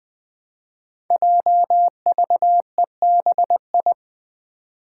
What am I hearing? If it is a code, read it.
Morse code, heard as a keyed tone, about 20 wpm, 700 Hz. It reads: JVEBI
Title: Five Challenging Letter-Number Combinations 20wpm